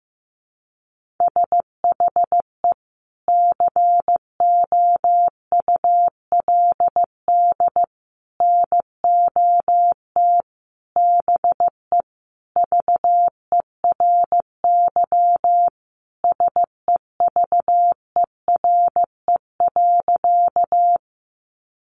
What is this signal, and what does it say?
Morse code, heard as a keyed tone, about 15 wpm, 700 Hz. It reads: SHE COULD NOT BE VERY SEVERE.
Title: Sentences from Top 1400 Words 15wpm